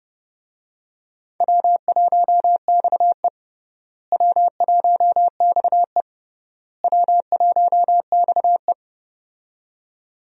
Morse code, heard as a keyed tone, about 31 wpm, 700 Hz. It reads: W1XE W1XE W1XE
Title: USA Call Signs - 4 Characters - Repeated 3x - 30wpm